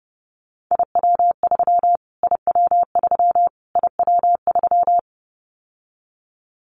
Morse code, heard as a keyed tone, about 30 wpm, 700 Hz. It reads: IW3 IW3 IW3
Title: Callsign Prefixes for Top 25 DXCC - Repeated 3x - 30wpm